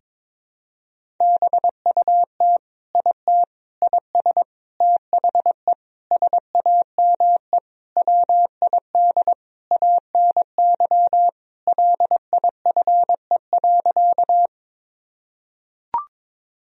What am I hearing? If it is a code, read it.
Morse code, heard as a keyed tone, about 22 wpm, 700 Hz. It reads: BUT IT IS THE SAME WID ANY LIFE.
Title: Sentences from Top 300 Words and Abbreviations 22wpm